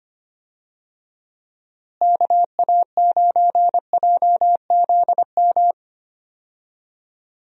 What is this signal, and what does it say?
Morse code, heard as a keyed tone, about 25 wpm, 700 Hz. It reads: KA9JZM
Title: USA Call Signs - 6 Characters - Repeated 2x - 25wpm